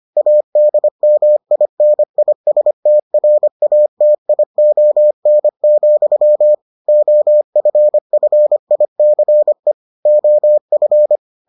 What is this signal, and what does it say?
Morse code, heard as a keyed tone, about 25 wpm, 600 Hz. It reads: ADMINISTRATION, OFFICE OF